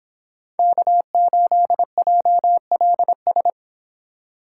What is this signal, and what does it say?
Morse code, heard as a keyed tone, about 26 wpm, 700 Hz. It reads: K8JLS
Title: Speed Racing - USA Call Signs 22wpm